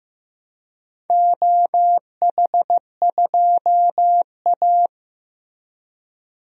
Morse code, heard as a keyed tone, about 15 wpm, 700 Hz. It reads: OH2A
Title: DXCC-100 Call Signs - 4 Characters - Repeated 3x - 15wpm